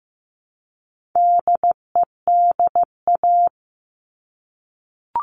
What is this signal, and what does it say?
Morse code, heard as a keyed tone, about 15 wpm, 700 Hz. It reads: DEDA
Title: All Cut Numbers - 4-Digit 15wpm